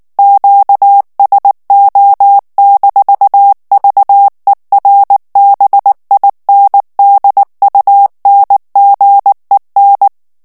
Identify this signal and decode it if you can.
Morse code, heard as a keyed tone, about 19 wpm, 800 Hz. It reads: QSO-VERBINDUNGEN
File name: morse mp3 / ALLEMAND texte clair 2 vitesse 18 mots minute